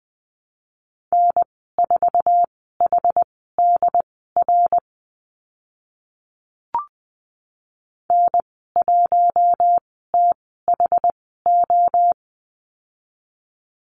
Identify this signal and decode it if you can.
Morse code, heard as a keyed tone, about 20 wpm, spaced as for 14 wpm, 700 Hz. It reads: N4HDR N1THO
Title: 069 Lesson 12g - Letter d - Call Signs 20wpm